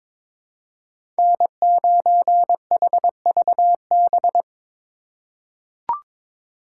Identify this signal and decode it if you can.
Morse code, heard as a keyed tone, about 22 wpm, 700 Hz. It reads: N9HVB
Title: USA Call Signs - 5 Characters - 4x Character Spacing - 22wpm